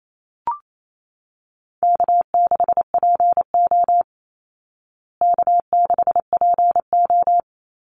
Morse code, heard as a keyed tone, about 28 wpm, 700 Hz. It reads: K6PO K6PO
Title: Easy Call Signs Repeated 2x - 28wpm